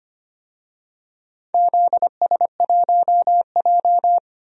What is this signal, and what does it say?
Morse code, heard as a keyed tone, about 25 wpm, 700 Hz. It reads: ZS1J
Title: DXCC-100 Call Signs - 4 Characters - 25wpm